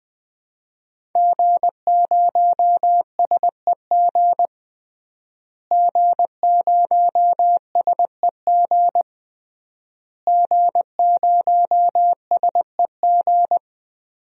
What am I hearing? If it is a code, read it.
Morse code, heard as a keyed tone, about 20 wpm, 700 Hz. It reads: G0SEG G0SEG G0SEG
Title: DXCC-100 Call Signs - 5 Characters - Repeated 3x - 20wpm